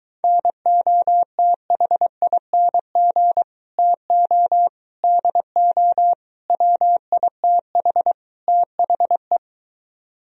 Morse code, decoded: NOTHING TO DO WITH THE

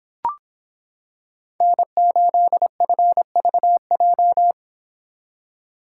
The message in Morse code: N8FVJ

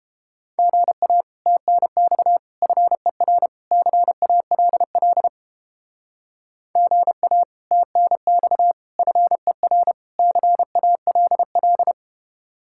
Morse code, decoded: GA TNX FER CALL GA TNX FER CALL

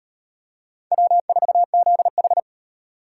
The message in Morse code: W4ZH

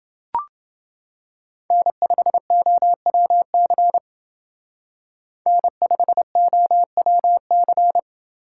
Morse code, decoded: N5OWC N5OWC

